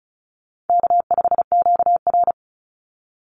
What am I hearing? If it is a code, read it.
Morse code, heard as a keyed tone, about 34 wpm, 700 Hz. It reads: K5QR
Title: NA SOTA Chaser Callsigns 35wpm